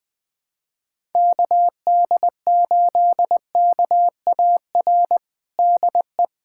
Morse code, heard as a keyed tone, about 20 wpm, 700 Hz. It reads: KD8KAR DE
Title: USA Callsign DE Callsign - 6 Characters - 20wpm